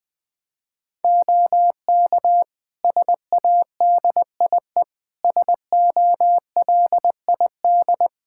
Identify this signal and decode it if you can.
Morse code, heard as a keyed tone, about 20 wpm, 700 Hz. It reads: OK SADIE SOLID
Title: Practice QSO - Exchange 3 - More Info - 20wpm